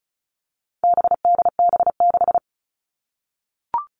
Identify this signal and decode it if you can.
Morse code, heard as a keyed tone, about 34 wpm, 700 Hz. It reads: BDB6
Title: All Cut Numbers - 4-Digit 35wpm